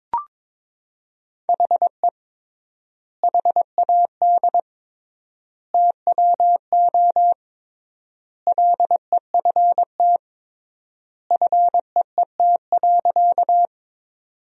Morse code, heard as a keyed tone, about 22 wpm, 700 Hz. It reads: HE HAD TWO LEFT FEET.
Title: 3x Word-Spacing - Sentences from Top 300 Words - 22wpm